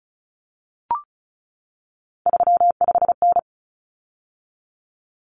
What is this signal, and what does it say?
Morse code, heard as a keyed tone, about 34 wpm, 700 Hz. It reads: 35N
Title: Common Cut Numbers - 3-Digit 35wpm